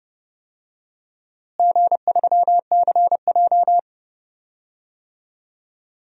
Morse code, heard as a keyed tone, about 29 wpm, 700 Hz. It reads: G3CJ
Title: DXCC-100 Call Signs - 4 Characters - Repeated 3x - 30wpm